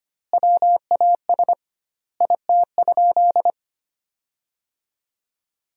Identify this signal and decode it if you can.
Morse code, heard as a keyed tone, about 25 wpm, 700 Hz. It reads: WAS IT?
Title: 2x Word-Spacing - Sentences from Top 100 Words - 25wpm